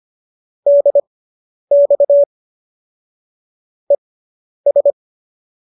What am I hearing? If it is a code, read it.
Morse code, heard as a keyed tone, about 25 wpm, spaced as for 10 wpm, 570 Hz. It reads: DX ES
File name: qso219 10